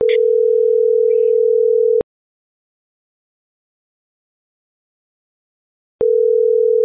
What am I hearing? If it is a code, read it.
Ringback tone (North America), 440 + 480 Hz.